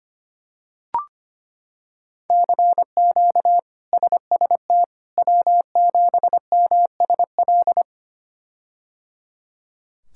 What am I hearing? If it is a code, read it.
Morse code, heard as a keyed tone, about 25 wpm, 700 Hz. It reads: CQ SST W7MSL
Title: K1USN SST QSO Practice - 25wpm